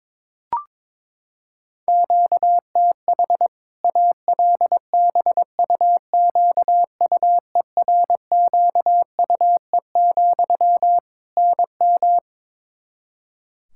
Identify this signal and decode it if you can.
Morse code, heard as a keyed tone, about 21 wpm, 700 Hz. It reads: QTH ALBUQUERQUE, NM
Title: Practice QSO - Exchange 1 - Beginning - 22wpm